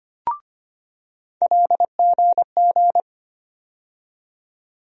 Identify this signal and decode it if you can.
Morse code, heard as a keyed tone, about 25 wpm, 700 Hz. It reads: LGG
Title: USA Call Sign Postfixes - 3 Letters 25wpm